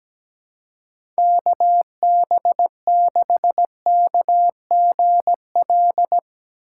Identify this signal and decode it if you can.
Morse code, heard as a keyed tone, about 17 wpm, 700 Hz. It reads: KB6KGL
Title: USA Call Signs - 6 Characters - 6x Character Spacing - 17wpm